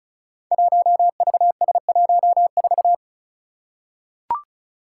Morse code, heard as a keyed tone, about 36 wpm, 700 Hz. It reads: 1VS14
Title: Five Challenging Letter-Number Combinations 35wpm